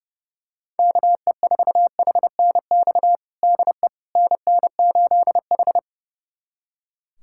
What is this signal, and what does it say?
Morse code, heard as a keyed tone, about 30 wpm, 700 Hz. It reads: KE4HNX DE NN8H
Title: USA Callsign DE Callsign 30wpm